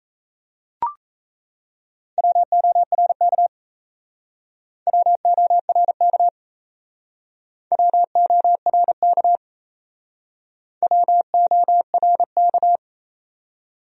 Morse code, heard as a keyed tone, about 34 wpm, 700 Hz. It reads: WORK WORK WORK WORK
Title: Speed-Racing Top 300 Words 28wpm